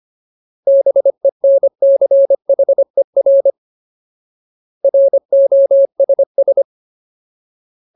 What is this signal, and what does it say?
Morse code, heard as a keyed tone, about 25 wpm, 550 Hz. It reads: BENCHER ROSS
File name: Ultimate Word List / 25WPM Ultimate